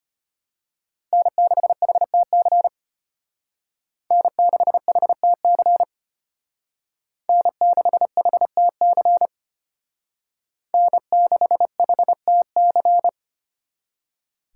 Morse code, heard as a keyed tone, about 32 wpm, 700 Hz. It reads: N6HTC N6HTC N6HTC N6HTC